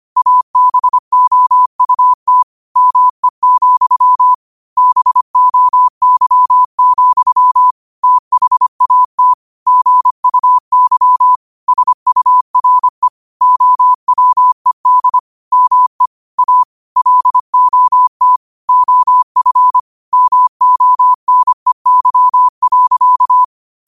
Morse code, heard as a keyed tone, about 25 wpm, 1000 Hz. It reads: ABOUT ME, BOY, THAT GUY SURE OWED ME A LOT OF MONEY.
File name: qotdfiles / 25wpm